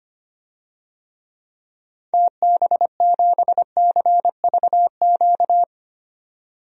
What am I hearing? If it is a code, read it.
Morse code, heard as a keyed tone, about 25 wpm, 700 Hz. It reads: TB7CVQ
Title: DXCC-100 Call Signs - 6 Characters - 25wpm